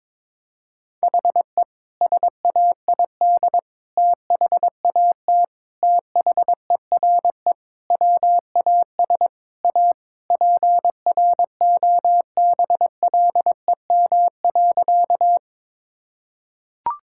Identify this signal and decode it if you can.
Morse code, heard as a keyed tone, about 22 wpm, 700 Hz. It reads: HE SAID THAT THERE WAS A PROBLEM.
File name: practice / 22wpm / Sentences with 3-Word Phrases - 22wpm